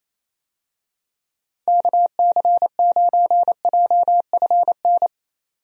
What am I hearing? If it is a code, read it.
Morse code, heard as a keyed tone, about 28 wpm, 700 Hz. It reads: KC9JFN
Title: USA Call Signs - 6 Characters - 28wpm